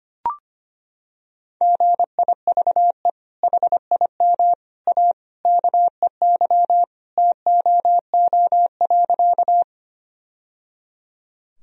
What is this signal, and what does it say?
Morse code, heard as a keyed tone, about 25 wpm, 700 Hz. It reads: GIVE HIM A KEY TOO.
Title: Sentences from Top 1000 Words - Review Part 4 - 25wpm